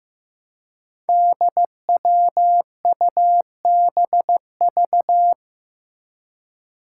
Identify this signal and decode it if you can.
Morse code, heard as a keyed tone, about 15 wpm, 700 Hz. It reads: DWUBV
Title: Five Challenging Letter-Number Combinations 15wpm